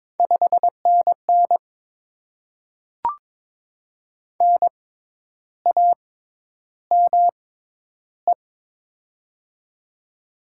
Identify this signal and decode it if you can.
Morse code, heard as a keyed tone, about 23 wpm, spaced as for 10 wpm, 700 Hz. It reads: 5NN NAME
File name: Top 20 QSO Elements - 6x Character Spacing - 22wpm